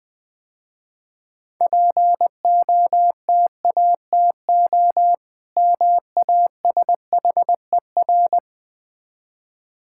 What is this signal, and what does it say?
Morse code, heard as a keyed tone, about 20 wpm, 700 Hz. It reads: POTATO MASHER
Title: Easy Words - Speed-Racing with Hints - 20wpm